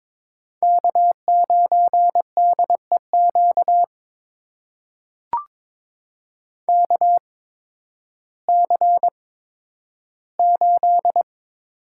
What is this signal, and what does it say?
Morse code, heard as a keyed tone, about 22 wpm, spaced as for 12 wpm, 700 Hz. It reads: K9DEQ KC8
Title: USA Call Signs - 8x Character Spacing - 22wpm